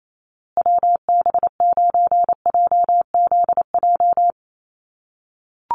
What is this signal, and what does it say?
Morse code, heard as a keyed tone, about 28 wpm, 700 Hz. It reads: WB9JZJ
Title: USA Call Signs Repeated 3x 28wpm